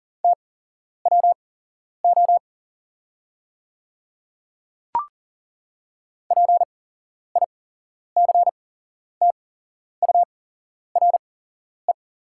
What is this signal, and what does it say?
Morse code, heard as a keyed tone, about 40 wpm, spaced as for 11 wpm, 690 Hz. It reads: TWO PICTURE
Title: 8x Character Spacing - Top 300 Words - 40wpm